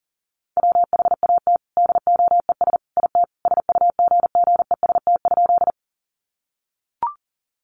Morse code, heard as a keyed tone, about 40 wpm, 700 Hz. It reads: WHAT DOES IT SUGGEST?
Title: Sentences from Top 1300 Words 40wpm